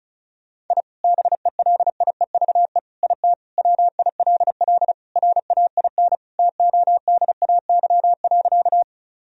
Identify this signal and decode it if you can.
Morse code, heard as a keyed tone, about 36 wpm, 700 Hz. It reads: I BELIEVE IT WILL RAIN TODAY.
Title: Sentences from Top 1000 Words - Review Part 1 - 35wpm